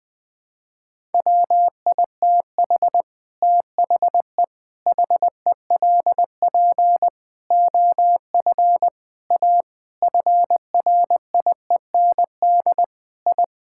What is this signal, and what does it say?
Morse code, heard as a keyed tone, about 19 wpm, 700 Hz. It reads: WITH THE HELP OF A FRIEND I
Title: Sentences with 5-Word Phrases - 20wpm